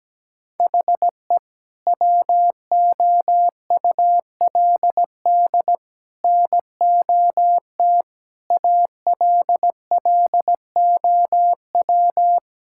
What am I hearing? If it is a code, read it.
Morse code, heard as a keyed tone, about 17 wpm, 700 Hz. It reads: HE WOULD NOT ALLOW
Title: Sentences from Top 1000 Words - Review Part 5 - 17wpm